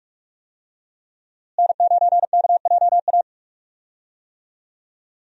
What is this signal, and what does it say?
Morse code, heard as a keyed tone, about 42 wpm, 700 Hz. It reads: N9KJA
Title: USA Call Signs - 5 Characters - Repeated 2x - 45wpm